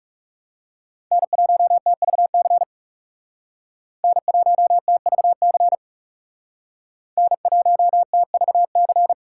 Morse code, heard as a keyed tone, about 38 wpm, 700 Hz. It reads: N1TVC N1TVC N1TVC